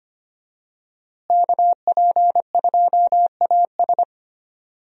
Morse code, heard as a keyed tone, about 25 wpm, 700 Hz. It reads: KP2AS
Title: USA Call Signs - 5 Characters - Repeated 3x - 25wpm